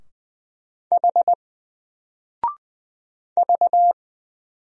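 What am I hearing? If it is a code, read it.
Morse code, heard as a keyed tone, about 20 wpm, 700 Hz. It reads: H V
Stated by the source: Single letter and number Morse drill at 20 wpm, from N4LCD.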